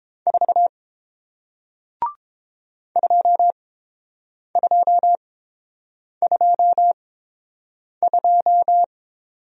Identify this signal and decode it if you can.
Morse code, heard as a keyed tone, about 30 wpm, spaced as for 7 wpm, 700 Hz. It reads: 4 2222